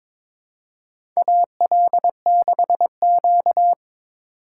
Morse code, heard as a keyed tone, about 22 wpm, 700 Hz. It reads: AL6Q
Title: USA Call Signs - 4 Characters - 6x Character Spacing - 22wpm